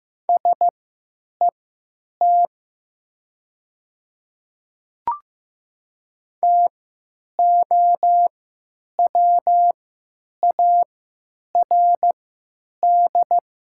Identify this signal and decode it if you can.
Morse code, heard as a keyed tone, about 15 wpm, spaced as for 9 wpm, 700 Hz. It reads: SET TOWARD
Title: 3x Character Spacing - Top 500 Words - 15wpm